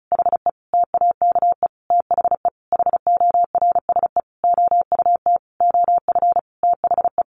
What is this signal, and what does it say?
Morse code, heard as a keyed tone, about 34 wpm, 700 Hz. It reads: HE TAKE THE HORSE OUT OF THE
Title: Sentences from Top 1500 Words 35wpm